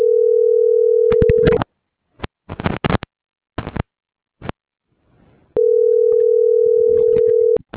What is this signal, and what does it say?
Ringback tone (North America), 440 + 480 Hz.